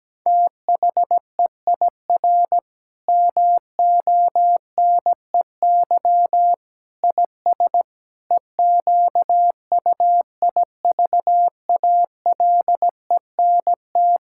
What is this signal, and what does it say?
Morse code, heard as a keyed tone, about 17 wpm, 700 Hz. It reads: THEIR MONEY IS EQUIVALENT
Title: Sentences from Top 1600 Words 17wpm